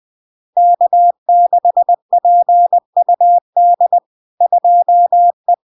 Morse code, heard as a keyed tone, about 20 wpm, 700 Hz. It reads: K6PUD 2E
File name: LARK-FD-2022-Log-in-CW-20wpm-700-Hz